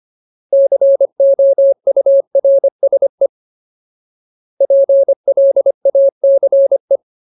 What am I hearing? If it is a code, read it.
Morse code, heard as a keyed tone, about 25 wpm, 550 Hz. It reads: COURSE PLACE
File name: Ultimate Word List / 25WPM Ultimate